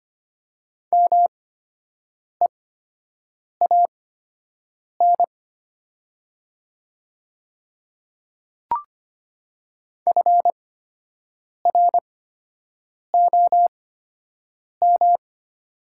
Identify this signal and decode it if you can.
Morse code, heard as a keyed tone, about 25 wpm, spaced as for 7 wpm, 700 Hz. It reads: MEAN FROM
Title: 8x Character Spacing - Top 200 Words - 25wpm